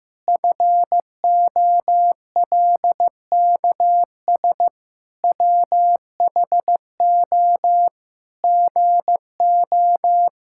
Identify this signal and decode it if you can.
Morse code, heard as a keyed tone, about 15 wpm, 700 Hz. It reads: FOLKS WHO GO
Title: Pangram Challenge v2 - 15wpm